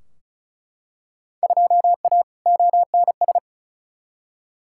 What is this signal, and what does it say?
Morse code, heard as a keyed tone, about 34 wpm, 700 Hz. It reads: 2A ONS
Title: Field Day Exchanges Encore - 35wpm